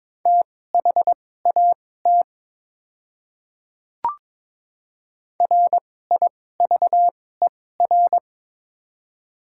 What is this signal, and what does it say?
Morse code, heard as a keyed tone, about 22 wpm, spaced as for 16 wpm, 700 Hz. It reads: THAT RIVER
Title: 2x Character Spacing - Top 500 Words - 22wpm